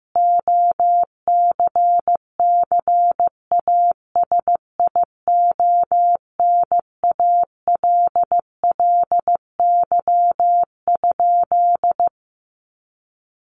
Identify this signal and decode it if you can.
Morse code, heard as a keyed tone, about 15 wpm, 700 Hz. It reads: OCCASIONALLY?
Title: Sentences from Top 1300 Words 15wpm